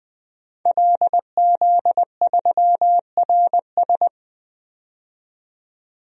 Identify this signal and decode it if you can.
Morse code, heard as a keyed tone, about 20 wpm, 700 Hz. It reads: LZ3RS
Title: DXCC-100 Call Signs - 5 Characters - 20wpm